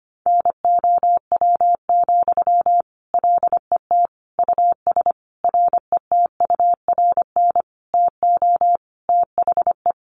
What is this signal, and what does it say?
Morse code, heard as a keyed tone, about 25 wpm, 700 Hz. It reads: NOW, LET US RETURN TO THE
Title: Sentences from Top 800 Words 25wpm